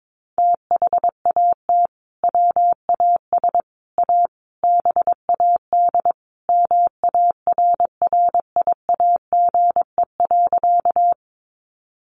Morse code, heard as a keyed tone, about 22 wpm, 700 Hz. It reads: THAT WAS A BAD MARRIAGE.
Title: Sentences from Top 800 Words 22wpm